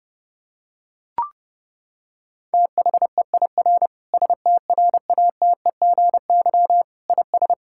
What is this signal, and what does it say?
Morse code, heard as a keyed tone, about 30 wpm, 700 Hz. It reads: THEIR STRATEGY IS
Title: Sentences from Top 1000 Words - Review Part 5 - 30wpm